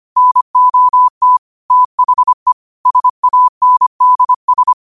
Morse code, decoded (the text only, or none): NOT THE SANDS